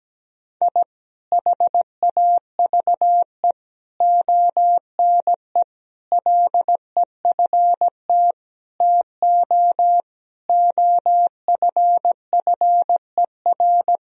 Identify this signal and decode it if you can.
Morse code, heard as a keyed tone, about 17 wpm, 700 Hz. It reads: I HAVE ONE LEFT TO OFFER
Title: Sentences from Top 1000 Words - Review Part 4 - 17wpm